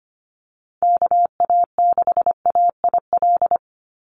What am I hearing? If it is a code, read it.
Morse code, heard as a keyed tone, about 25 wpm, 700 Hz. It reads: KA6AIL
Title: Call Signs - 25wpm